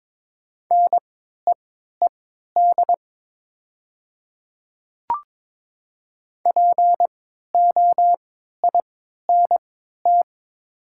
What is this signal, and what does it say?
Morse code, heard as a keyed tone, about 22 wpm, spaced as for 12 wpm, 700 Hz. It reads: NEED POINT
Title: 3x Character Spacing - Top 200 Words - 22wpm